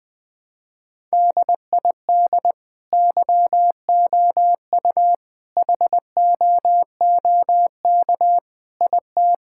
Morse code, heard as a keyed tone, about 21 wpm, 700 Hz. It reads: DID YOU HOOK IT
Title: Sentences from Top 1600 Words 20wpm